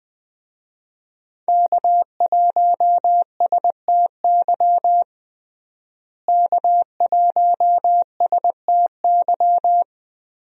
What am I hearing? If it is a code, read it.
Morse code, heard as a keyed tone, about 20 wpm, 700 Hz. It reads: K1STY K1STY
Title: USA Call Signs - 5 Characters - Repeated 3x - 20wpm